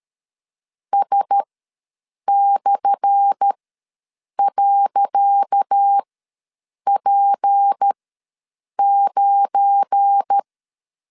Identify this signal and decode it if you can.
Morse code, heard as a keyed tone, about 13 wpm, spaced as for 7 wpm, 800 Hz. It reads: S/.P9
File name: LEZIONI / 10 SIGNS / 10C DOT ws 12wpm 3548